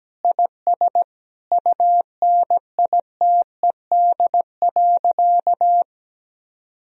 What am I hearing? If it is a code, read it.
Morse code, heard as a keyed tone, about 17 wpm, 700 Hz. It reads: IS UNITED.
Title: Sentences from Top 1000 Words - Review Part 4 - 17wpm